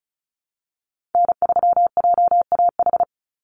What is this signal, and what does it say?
Morse code, heard as a keyed tone, about 34 wpm, 700 Hz. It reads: N3JAH
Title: USA Call Signs 35wpm